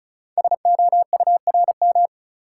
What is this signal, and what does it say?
Morse code, heard as a keyed tone, about 33 wpm, 690 Hz. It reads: SOURM
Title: Code Groups - 5 Letters - 35wpm